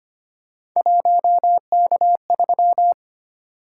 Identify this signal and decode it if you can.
Morse code, heard as a keyed tone, about 25 wpm, 700 Hz. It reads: 1K3
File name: Three Letter-Number Combinations - 25wpm